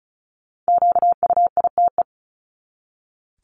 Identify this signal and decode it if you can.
Morse code, heard as a keyed tone, about 34 wpm, 700 Hz. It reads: QUITE
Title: Top 501-750 Words 35wpm